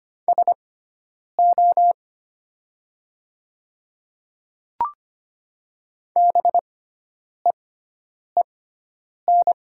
Morse code, heard as a keyed tone, about 25 wpm, spaced as for 9 wpm, 700 Hz. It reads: SO BEEN
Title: 6x Character Spacing - Top 100 Words - 25wpm